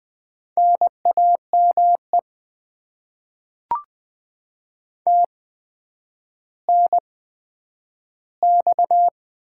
Morse code, heard as a keyed tone, about 21 wpm, spaced as for 10 wpm, 700 Hz. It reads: NAME TNX
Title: Top 20 QSO Elements - 8x Character Spacing - 20wpm